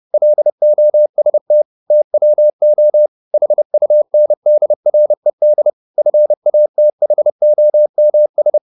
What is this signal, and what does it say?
Morse code, heard as a keyed tone, about 30 wpm, 600 Hz. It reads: LOST TWO HUNDRED FATHOMS
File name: the old man and the sea / the old man and the sea 1 30wpm